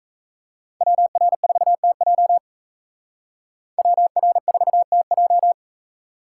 Morse code, heard as a keyed tone, about 33 wpm, 700 Hz. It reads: WR4TJ WR4TJ
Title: Speed Racing - USA 2x2 Call Signs 28wpm